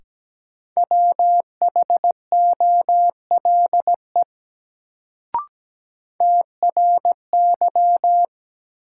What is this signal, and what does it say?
Morse code, heard as a keyed tone, about 17 wpm, 700 Hz. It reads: WHOLE TRY
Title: Sets of 2 Words - New Word Drill - 17wpm